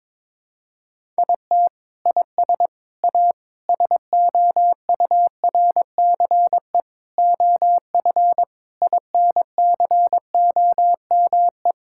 Morse code, heard as a keyed tone, about 22 wpm, 700 Hz. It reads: IT IS A SOURCE OF INCOME
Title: Sentences from Top 1000 Words - Review Part 5 - 22wpm